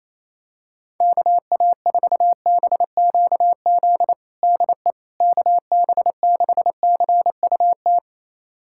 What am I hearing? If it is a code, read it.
Morse code, heard as a keyed tone, about 28 wpm, 700 Hz. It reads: KA4BQZ DE KB6CUT
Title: USA Callsign DE Callsign - 6 Characters - 28wpm